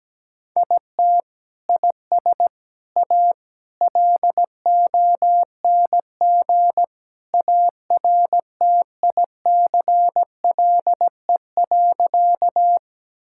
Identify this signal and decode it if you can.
Morse code, heard as a keyed tone, about 17 wpm, 700 Hz. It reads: IT IS A LONG ARTICLE.
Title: Sentences from Top 1000 Words - Review Part 5 - 17wpm